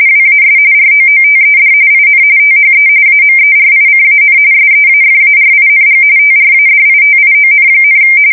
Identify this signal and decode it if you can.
RTTY, about 45.45 baud. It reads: WAS ROOTED IN THE AMERICAN SEAFARING TRADITION,
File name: fsk170 / usnavy1